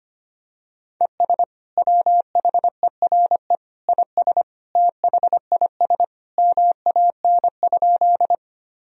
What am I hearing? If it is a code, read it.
Morse code, heard as a keyed tone, about 25 wpm, 700 Hz. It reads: ES WHERE IS THIS MAN?
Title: Sentences from Top 200 Words and Abbreviations 25wpm